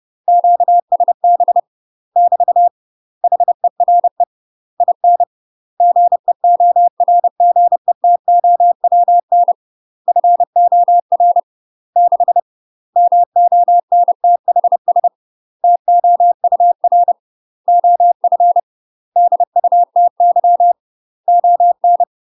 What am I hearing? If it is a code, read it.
Morse code, heard as a keyed tone, about 30 wpm, 700 Hz. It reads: QSB <BT> HERE IN GEORGETOWN FOR 6 MONTHS TOUR OF DUTY ON
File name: UK MASTER QSOs TRAINING / 30WPM / UK MASTER QSOs 0022